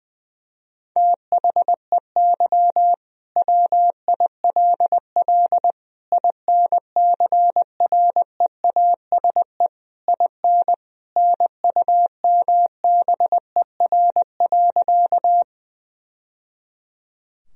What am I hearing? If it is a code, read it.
Morse code, heard as a keyed tone, about 20 wpm, 700 Hz. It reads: THEY WILL INCREASE IN NUMBER.
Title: Sentences from Top 1000 Words - Review Part 4 - 20wpm